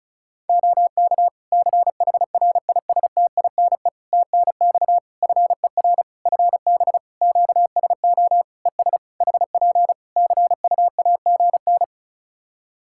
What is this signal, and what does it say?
Morse code, heard as a keyed tone, about 36 wpm, 700 Hz. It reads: OK CHRISTINE TNX FER FB QSO ES HP CUAGN
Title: Practice QSO - Exchange 4 - Ending - 35wpm